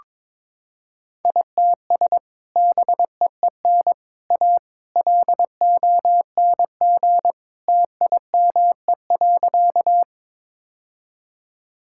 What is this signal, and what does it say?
Morse code, heard as a keyed tone, about 22 wpm, 700 Hz. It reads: ITS BEEN A LONG TIME.
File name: practice / 22wpm / Sentences with 3-Word Phrases - 22wpm